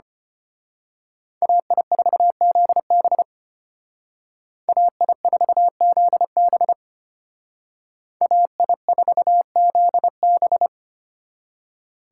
Morse code, heard as a keyed tone, about 29 wpm, 700 Hz. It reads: AI4ZB AI4ZB AI4ZB
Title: Speed Racing - USA 2x2 Call Signs 25wpm